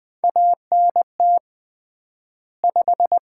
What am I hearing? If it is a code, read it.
Morse code, heard as a keyed tone, about 21 wpm, 700 Hz. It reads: ANT 5
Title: Common QSO Phrases - 3x Word-Spacing - 20wpm